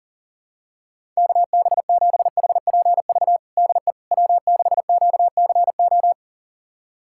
Morse code, heard as a keyed tone, about 40 wpm, 700 Hz. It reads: KB7HPV DE W6QCO
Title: USA Callsign DE Callsign 40wpm